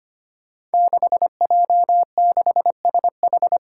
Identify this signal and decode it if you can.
Morse code, heard as a keyed tone, about 25 wpm, 700 Hz. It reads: 6J6SH
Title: Five Challenging Letter-Number Combinations 25wpm